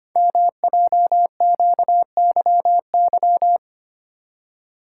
Morse code, heard as a keyed tone, about 26 wpm, 700 Hz. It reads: MJQYY
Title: Code Groups - 5 Letters - 25wpm